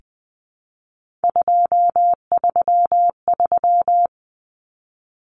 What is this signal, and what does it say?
Morse code, heard as a keyed tone, about 20 wpm, 700 Hz. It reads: 233
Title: Numbers - 3-Digit 20wpm